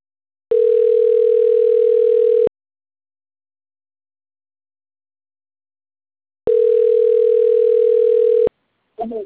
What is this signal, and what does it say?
Ringback tone (North America), 440 + 480 Hz.
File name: autotestagent / 10-30-48.136..callcentertransfer. .f2bbf5a2-b030-49f8-b047